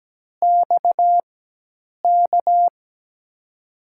Morse code, heard as a keyed tone, about 17 wpm, spaced as for 8 wpm, 700 Hz. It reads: XK
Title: USA Call Signs - 6 Characters - 4x Character Spacing - 17wpm